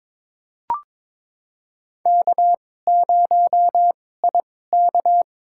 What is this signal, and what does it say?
Morse code, heard as a keyed tone, about 21 wpm, spaced as for 16 wpm, 700 Hz. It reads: K0IK
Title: USA Call Signs - 4 Characters - 2x Character Spacing - 22wpm